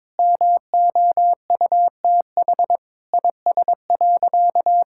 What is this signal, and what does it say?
Morse code, heard as a keyed tone, about 22 wpm, 700 Hz. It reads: MOUTH IS.
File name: Sentences with 7-Word Phrases - 22wpm